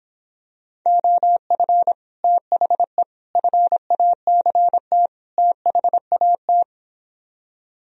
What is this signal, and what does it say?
Morse code, heard as a keyed tone, about 26 wpm, 700 Hz. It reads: OF THE FACT THAT